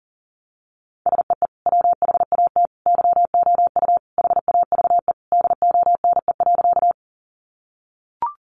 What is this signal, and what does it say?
Morse code, heard as a keyed tone, about 40 wpm, 700 Hz. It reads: SEE WHAT YOU HAVE DONE.
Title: Sentences from Top 500 Words 40wpm